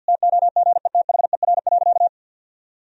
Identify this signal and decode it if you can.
Morse code, heard as a keyed tone, about 50 wpm, 700 Hz. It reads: TOGETHER.
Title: Sentences from Top 1000 Words - Review Part 1 - 50wpm